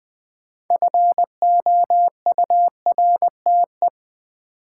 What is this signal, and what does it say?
Morse code, heard as a keyed tone, about 20 wpm, 700 Hz. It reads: FOURTE
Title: Build A Word - 3 Letters At A Time - 8-Letter Words - 20wpm